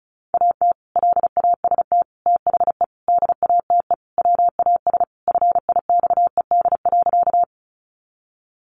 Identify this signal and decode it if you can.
Morse code, heard as a keyed tone, about 34 wpm, 700 Hz. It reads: AT LAST THE DATE WAS FIXED.
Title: Sentences from Top 1100 Words 35wpm